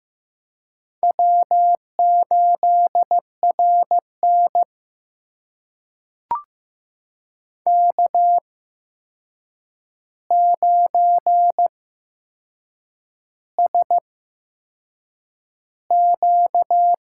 Morse code, heard as a keyed tone, about 15 wpm, spaced as for 6 wpm, 700 Hz. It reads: W8RN K9SQ
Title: USA Call Signs - 4 Characters - 8x Character Spacing - 15wpm